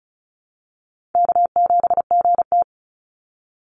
Morse code, heard as a keyed tone, about 34 wpm, 700 Hz. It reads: K7GT
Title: Call Signs - 35wpm